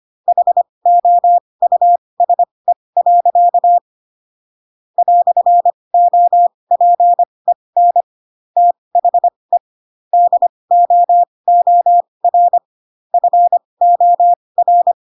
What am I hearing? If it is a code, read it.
Morse code, heard as a keyed tone, about 25 wpm, spaced as for 20 wpm, 700 Hz. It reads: HOUSE. "OPEN THE DOOR FOR
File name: Pinocchio / Pinocchio ch7 15 25 ik1hge